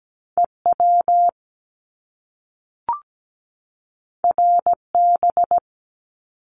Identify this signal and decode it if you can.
Morse code, heard as a keyed tone, about 17 wpm, 700 Hz. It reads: EW RB
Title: Top Two Letter Combinations in Words 17wpm